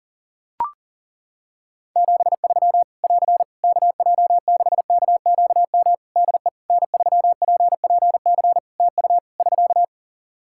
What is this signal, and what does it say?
Morse code, heard as a keyed tone, about 40 wpm, 700 Hz. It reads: 73 <AR> KJ6KQM DE N3PPC TU <SK>
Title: K1USN SST QSO Practice 40wpm